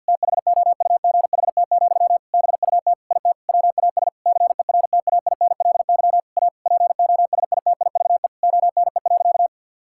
Morse code, heard as a keyed tone, about 50 wpm, 700 Hz. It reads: THOUGHT, BUT IT WAS CERTAINLY A POSITIVE ONE.
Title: Sentences from Top 1000 Words - Review Part 6 - 50wpm